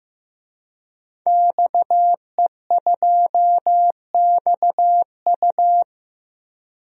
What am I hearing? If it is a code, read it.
Morse code, heard as a keyed tone, about 15 wpm, 700 Hz. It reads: XE2XU
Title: DXCC-100 Call Signs - 5 Characters - 15wpm